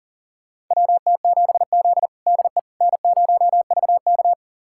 Morse code, heard as a keyed tone, about 40 wpm, 700 Hz. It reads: WT7Z DE N0VK
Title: USA Callsign DE Callsign - 4 Characters - 40wpm